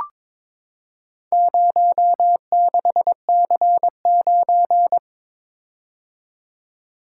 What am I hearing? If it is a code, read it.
Morse code, heard as a keyed tone, about 22 wpm, 700 Hz. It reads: 06C9
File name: practice / 22wpm / Four Letter-Number Combinations - 22wpm